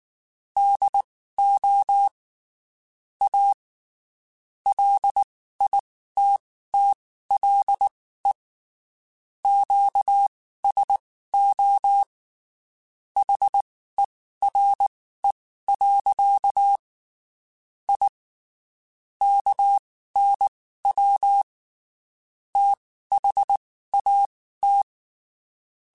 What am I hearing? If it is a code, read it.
Morse code, heard as a keyed tone, about 19 wpm, spaced as for 14 wpm, 800 Hz. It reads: DO A LITTLE QSO HERE. I KNW THAT
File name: (spaced) / QSOs Spaced-mp3s / WD0FBT @18 sp